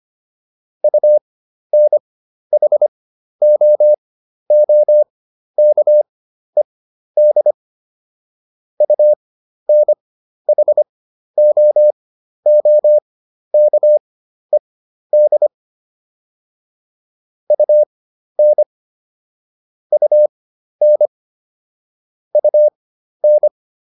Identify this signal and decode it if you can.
Morse code, heard as a keyed tone, about 25 wpm, spaced as for 12 wpm, 600 Hz. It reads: UNHOOKED UNHOOKED UN UN UN